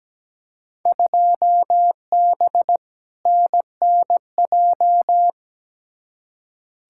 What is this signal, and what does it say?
Morse code, heard as a keyed tone, about 17 wpm, 700 Hz. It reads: 2B NNJ
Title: Field Day Exchanges Encore - 17wpm